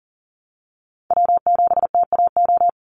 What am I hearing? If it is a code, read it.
Morse code, heard as a keyed tone, about 41 wpm, 700 Hz. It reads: W7TAO
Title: NA SOTA Chaser Callsigns 40wpm